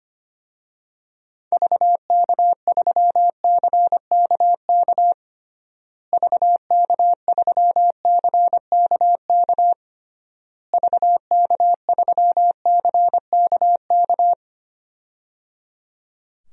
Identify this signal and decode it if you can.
Morse code, heard as a keyed tone, about 25 wpm, 700 Hz. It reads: VK3CKK VK3CKK VK3CKK
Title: DXCC-100 Call Signs - 6 Characters - Repeated 3x - 25wpm